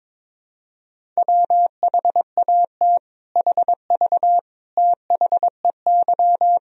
Morse code, heard as a keyed tone, about 22 wpm, 700 Hz. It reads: WHAT HV THEY
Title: Sentences from Top 500 Words and Abbreviations 22wpm